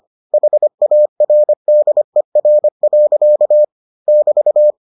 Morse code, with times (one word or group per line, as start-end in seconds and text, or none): 0.3-3.6 HARDER.
4.1-4.7 <BT>